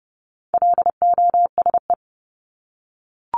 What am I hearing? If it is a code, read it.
Morse code, heard as a keyed tone, about 30 wpm, 700 Hz. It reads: LOSE
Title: Top 1001-1250 Words 30wpm